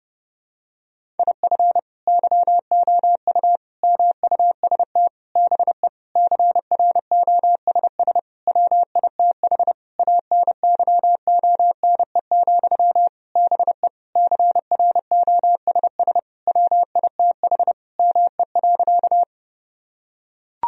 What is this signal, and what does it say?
Morse code, heard as a keyed tone, about 30 wpm, 700 Hz. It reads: IF YOU MUST BE CROSS WITH ANYONE, BE CROSS WITH ME.
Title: Sentences from Top 1000 Words - Review Part 1 - 30wpm